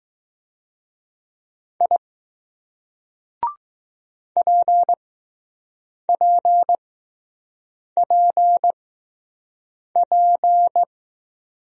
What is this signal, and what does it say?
Morse code, heard as a keyed tone, about 20 wpm, spaced as for 6 wpm, 700 Hz. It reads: I PPPP